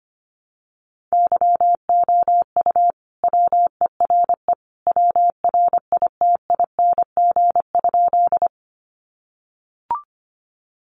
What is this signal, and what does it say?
Morse code, heard as a keyed tone, about 25 wpm, 700 Hz. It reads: YOU WERE WRITING?
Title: Sentences from Top 800 Words 25wpm